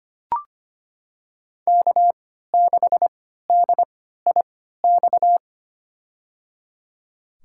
Morse code, heard as a keyed tone, about 25 wpm, spaced as for 14 wpm, 700 Hz. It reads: K6DIX
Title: USA Call Signs - 3x Character Spacing - 25wpm